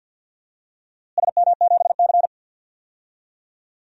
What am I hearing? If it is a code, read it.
Morse code, heard as a keyed tone, about 48 wpm, 700 Hz. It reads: SM7C
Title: DXCC-100 Call Signs - 4 Characters - Repeated 3x - 50wpm